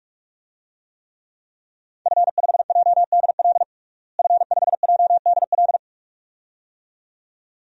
Morse code, heard as a keyed tone, about 41 wpm, 700 Hz. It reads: F5JDL F5JDL
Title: DXCC-100 Call Signs - 5 Characters - Repeated 2x - 45wpm